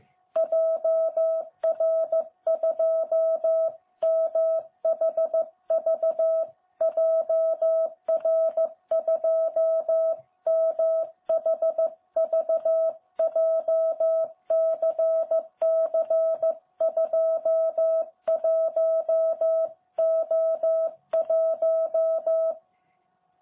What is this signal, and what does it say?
Morse code, heard as a keyed tone, about 14 wpm, 660 Hz. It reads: JR2MHV JR2MHV JCC21O1
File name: jh8cbh / 2021cw